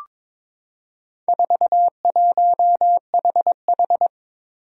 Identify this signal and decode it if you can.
Morse code, heard as a keyed tone, about 22 wpm, 700 Hz. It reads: 41HH
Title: Four Challenging Letter-Number Combinations 22wpm